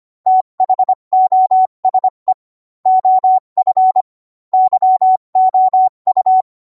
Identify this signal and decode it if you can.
Morse code, heard as a keyed tone, about 25 wpm, 750 Hz. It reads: THOSE OF YOU